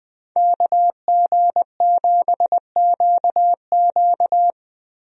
Morse code, heard as a keyed tone, about 21 wpm, 700 Hz. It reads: KG7QQ
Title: USA Call Signs - 5 Characters - Repeated 2x - 20wpm